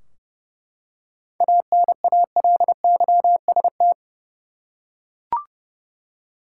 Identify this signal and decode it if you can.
Morse code, heard as a keyed tone, about 32 wpm, 700 Hz. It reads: ANALYST
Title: K1USN SST QSO Practice 30wpm